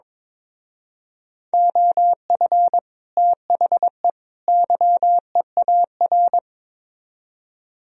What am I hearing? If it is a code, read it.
Morse code, heard as a keyed tone, about 24 wpm, 700 Hz. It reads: OF THE YEAR
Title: Speed-Racing - Sets of 3 Words 22wpm